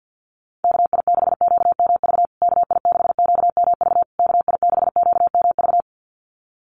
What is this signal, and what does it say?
Morse code, heard as a keyed tone, about 48 wpm, 700 Hz. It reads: KI6QMV KI6QMV KI6QMV
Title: USA Call Signs Repeated 3x 50wpm